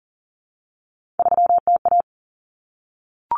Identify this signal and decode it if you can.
Morse code, heard as a keyed tone, about 40 wpm, 700 Hz. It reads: 3TA
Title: Common Cut Numbers - 3-Digit 40wpm